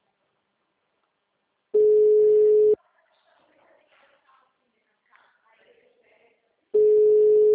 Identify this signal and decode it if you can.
Ringback tone (Europe, 425 Hz).
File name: out- -2003- - - .80